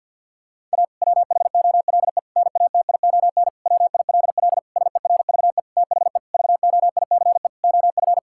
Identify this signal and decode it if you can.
Morse code, heard as a keyed tone, about 52 wpm, 700 Hz. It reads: A WHOLE NATION WILL SERVE THE VOICE OF
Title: Sentences from Top 1000 Words - Review Part 6 - 50wpm